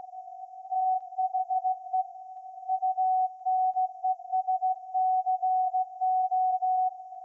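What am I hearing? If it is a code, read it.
Morse code, heard as a keyed tone, about 17 wpm, 750 Hz. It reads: THE UNESCO
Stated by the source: Morse code from SAQ, received at a station in Putignano, Apulia, Italy, on VLF.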